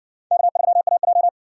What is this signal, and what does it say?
Morse code, heard as a keyed tone, about 60 wpm, 700 Hz. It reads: K3AJ